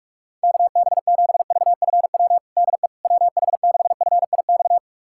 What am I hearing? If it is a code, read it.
Morse code, heard as a keyed tone, about 46 wpm, 700 Hz. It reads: KB7VFW DE WH6FIX
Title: USA Callsign DE Callsign - 6 Characters - 45wpm